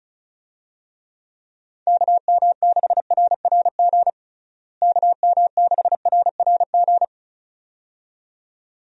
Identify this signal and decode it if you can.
Morse code, heard as a keyed tone, about 34 wpm, 700 Hz. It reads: KM6RRG KM6RRG
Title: USA Call Signs - 6 Characters - Repeated 2x - 35wpm